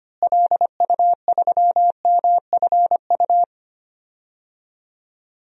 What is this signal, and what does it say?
Morse code, heard as a keyed tone, about 25 wpm, 700 Hz. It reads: LU3MFU
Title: DXCC-100 Call Signs - 6 Characters - Repeated 2x - 25wpm